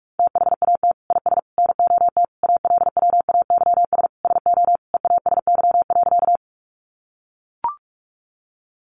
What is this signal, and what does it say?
Morse code, heard as a keyed tone, about 44 wpm, 700 Hz. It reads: THAT IS NOT ALWAYS SO EASY.
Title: Sentences from Top 700 Words 45wpm